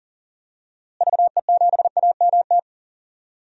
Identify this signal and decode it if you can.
Morse code, heard as a keyed tone, about 40 wpm, 700 Hz. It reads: VE7AMT
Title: DXCC-100 Call Signs - 6 Characters - Repeated 2x - 40wpm